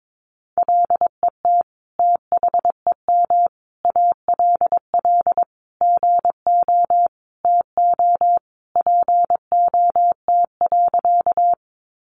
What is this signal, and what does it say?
Morse code, heard as a keyed tone, about 22 wpm, 700 Hz. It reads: LET THEM ALL GO TO POT.
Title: Sentences from Top 1000 Words 22wpm